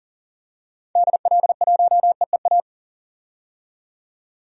Morse code, heard as a keyed tone, about 40 wpm, 700 Hz. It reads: DL1EEA
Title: DXCC-100 Call Signs - 6 Characters - 40wpm